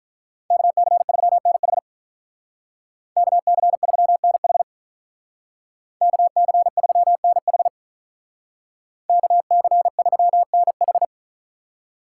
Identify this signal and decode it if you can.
Morse code, heard as a keyed tone, about 43 wpm, 700 Hz. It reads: KC3NH KC3NH KC3NH KC3NH
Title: Speed Racing - USA 2x2 Call Signs 35wpm